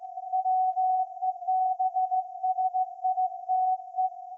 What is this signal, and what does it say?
Morse code, heard as a keyed tone, about 16 wpm, 750 Hz. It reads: WEBSITE
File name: iz7slz / VLF / SAQ 30-06-2024 0900